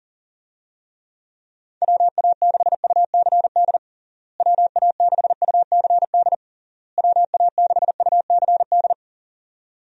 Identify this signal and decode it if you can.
Morse code, heard as a keyed tone, about 40 wpm, 700 Hz. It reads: WA6UCD WA6UCD WA6UCD
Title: USA Call Signs - 6 Characters - Repeated 3x - 40wpm